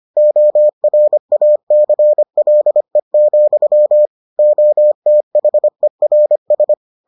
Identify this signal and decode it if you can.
Morse code, heard as a keyed tone, about 25 wpm, 600 Hz. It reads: ORACLE, OTHERS